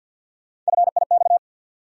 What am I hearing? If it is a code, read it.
Morse code, heard as a keyed tone, about 52 wpm, 700 Hz. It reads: FIX
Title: Sentences from Top 1400 Words - New Word Drill Alt - 50wpm